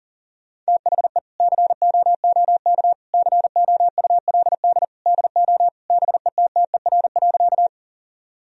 Morse code, heard as a keyed tone, about 40 wpm, 700 Hz. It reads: THE COOK COULD DO BETTER.
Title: Sentences from Top 1000 Words - Review Part 4 - 40wpm